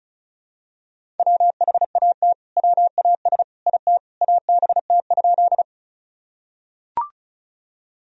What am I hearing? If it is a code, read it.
Morse code, heard as a keyed tone, about 36 wpm, 700 Hz. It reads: WHAT WAS IT ABT?
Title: Sentences from Top 100 Words and Abbreviations 35wpm